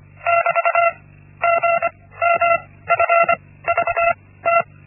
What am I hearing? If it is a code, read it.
Morse code, heard as a keyed tone, about 25 wpm, spaced as for 16 wpm, 680 Hz. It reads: <BT> GMFVT
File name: Linksmakalnis-1